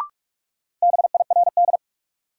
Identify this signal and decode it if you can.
Morse code, heard as a keyed tone, about 46 wpm, 700 Hz. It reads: BIRD
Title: Sentences from Top 400 Words - New Word Drill - 45wpm